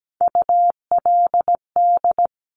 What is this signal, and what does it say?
Morse code, heard as a keyed tone, about 17 wpm, 700 Hz. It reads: ULD
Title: Top Three Letter Combinations in Words 17wpm